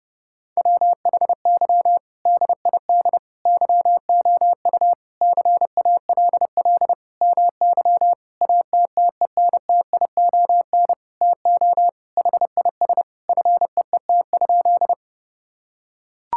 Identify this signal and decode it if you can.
Morse code, heard as a keyed tone, about 32 wpm, 700 Hz. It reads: WHY DID YOU CALL MY ATTENTION TO HIS FEET?
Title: Sentences from Top 1000 Words - Review Part 6 - 30wpm